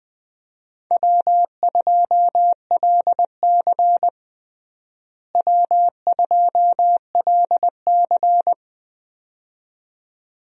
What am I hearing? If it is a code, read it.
Morse code, heard as a keyed tone, about 21 wpm, 700 Hz. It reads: W2LC W2LC
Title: USA Call Signs - 4 Characters - Repeated 2x - 20wpm